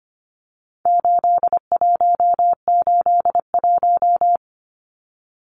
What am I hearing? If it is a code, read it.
Morse code, heard as a keyed tone, about 25 wpm, 700 Hz. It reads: 8181
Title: Numbers - 4-Digit 25wpm